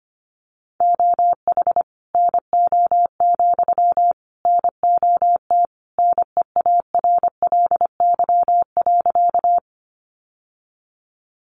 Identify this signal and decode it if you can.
Morse code, heard as a keyed tone, about 25 wpm, 700 Hz. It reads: OH NO, NOT NEARLY.
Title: Sentences from Top 1300 Words 25wpm